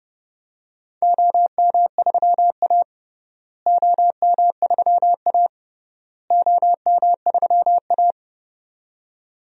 Morse code, heard as a keyed tone, about 30 wpm, 700 Hz. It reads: OM3A OM3A OM3A
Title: DXCC-100 Call Signs - 4 Characters - Repeated 3x - 30wpm